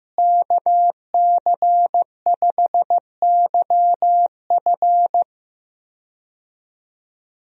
Morse code, heard as a keyed tone, about 15 wpm, 700 Hz. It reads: KC5YF
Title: USA Call Signs - 5 Characters - Repeated 2x - 15wpm